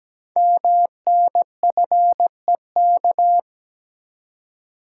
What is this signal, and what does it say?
Morse code, heard as a keyed tone, about 17 wpm, 700 Hz. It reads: MNFEK
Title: Code Groups - 5 Letters - 17wpm